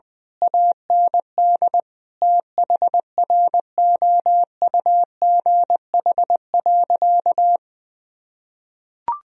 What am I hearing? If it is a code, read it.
Morse code, heard as a keyed tone, about 20 wpm, 700 Hz. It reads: AND THROUGH.
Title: Sentences from Top 1000 Words - Review Part 1 - 20wpm